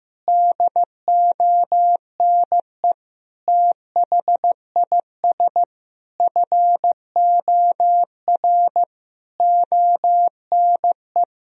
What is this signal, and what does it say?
Morse code, heard as a keyed tone, about 15 wpm, 700 Hz. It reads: DONE THIS FOR ONE
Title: Sentences from Top 1000 Words - Review Part 1 - 15wpm